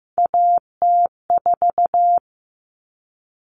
Morse code, heard as a keyed tone, about 15 wpm, 700 Hz. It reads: AT4
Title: Common Cut Numbers - 3-Digit 15wpm